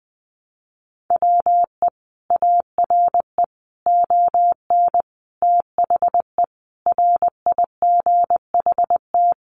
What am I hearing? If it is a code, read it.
Morse code, heard as a keyed tone, about 20 wpm, 700 Hz. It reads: WE ARE ON THE RIGHT
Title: Sentences from Top 1100 Words 20wpm